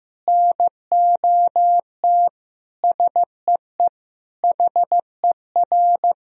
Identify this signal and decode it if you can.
Morse code, heard as a keyed tone, about 15 wpm, 700 Hz. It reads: NOT SEE HER
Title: Sentences from Top 1000 Words - Review Part 2 - 15wpm